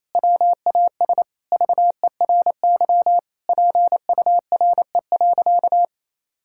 Morse code, heard as a keyed tone, about 28 wpm, 700 Hz. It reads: WAS VERY PURE.
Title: Sentences from Top 1600 Words 28wpm